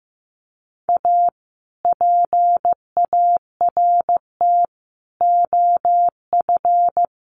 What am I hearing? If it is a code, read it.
Morse code, heard as a keyed tone, about 14 wpm, 700 Hz. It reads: A PART OF
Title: Sets of 4 Words 15wpm